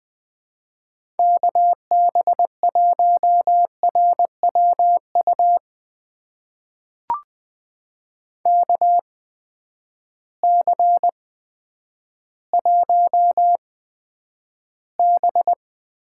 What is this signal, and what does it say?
Morse code, heard as a keyed tone, about 20 wpm, spaced as for 10 wpm, 700 Hz. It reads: KB1RWU KC1B
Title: USA Call Signs - 6 Characters - 8x Character Spacing - 20wpm